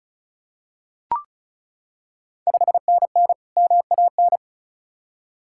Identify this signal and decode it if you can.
Morse code, heard as a keyed tone, about 36 wpm, 700 Hz. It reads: 5NN MAN
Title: Ontario QSO Party Exchange 35wpm